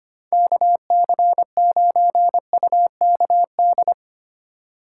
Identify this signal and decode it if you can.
Morse code, heard as a keyed tone, about 25 wpm, 700 Hz. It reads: KC9UKD
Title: USA Call Signs - 3x Character Spacing - 25wpm